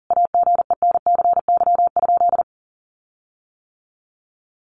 Morse code, heard as a keyed tone, about 40 wpm, 700 Hz. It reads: AGENCY?
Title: Sentences from Top 1300 Words 40wpm